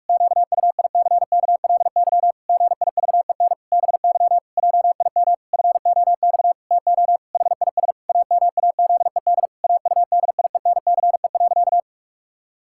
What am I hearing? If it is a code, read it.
Morse code, heard as a keyed tone, about 46 wpm, 700 Hz. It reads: QUICKLY GIVEN BY JIM FOX TO HIS AMAZED AUDIENCE.
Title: Pangram Challenge v2 - 45wpm